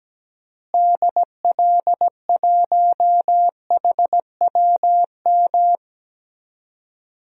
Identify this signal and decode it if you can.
Morse code, heard as a keyed tone, about 17 wpm, 700 Hz. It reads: DL1HWM
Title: DXCC-100 Call Signs - 6 Characters - Repeated 3x - 17wpm